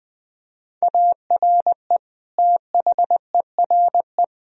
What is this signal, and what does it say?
Morse code, heard as a keyed tone, about 20 wpm, 700 Hz. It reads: ARE THERE
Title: Sentences from Top 1000 Words - Review Part 4 - 20wpm